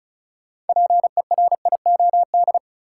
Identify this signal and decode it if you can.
Morse code, heard as a keyed tone, about 36 wpm, 700 Hz. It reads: PERIOD
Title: Sentences from Top 800 Words - New Word Drill - 35wpm